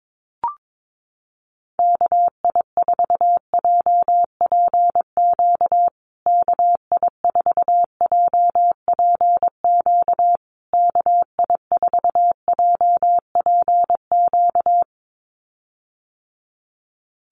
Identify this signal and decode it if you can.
Morse code, heard as a keyed tone, about 22 wpm, 700 Hz. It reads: KI4JPQ KI4JPQ KI4JPQ
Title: USA Call Signs Repeated 3x 22wpm